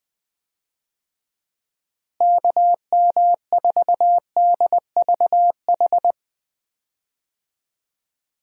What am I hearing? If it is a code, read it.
Morse code, heard as a keyed tone, about 21 wpm, 700 Hz. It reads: KM4DVH
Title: USA Call Signs - 6 Characters - Repeated 3x - 20wpm